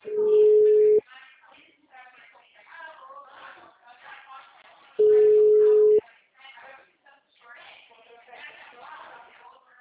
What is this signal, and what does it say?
Ringback tone (Europe, 425 Hz), with speech.